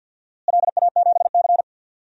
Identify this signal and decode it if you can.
Morse code, heard as a keyed tone, about 53 wpm, 700 Hz. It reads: LA7C
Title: DXCC-100 Call Signs - 4 Characters - Repeated 2x - 50wpm